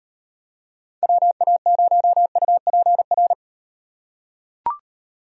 Morse code, heard as a keyed tone, about 38 wpm, 700 Hz. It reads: WA0UPR